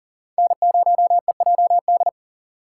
Morse code, heard as a keyed tone, about 51 wpm, 700 Hz. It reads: N0EJD